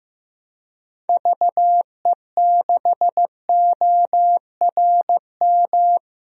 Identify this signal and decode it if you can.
Morse code, heard as a keyed tone, about 15 wpm, 690 Hz. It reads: VE6ORM
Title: DXCC-100 Call Signs - 6 Characters - 15wpm